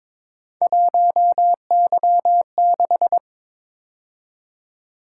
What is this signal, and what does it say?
Morse code, heard as a keyed tone, about 26 wpm, 700 Hz. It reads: 1Y6
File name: Speed-Racing - Three Letter-Number Combinations - 22wpm